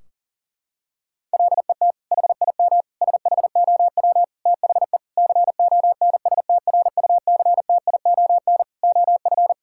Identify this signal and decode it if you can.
Morse code, heard as a keyed tone, about 40 wpm, 700 Hz. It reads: LET HIM SHOW THE CONSTRUCTION OF
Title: Sentences from Top 1600 Words 40wpm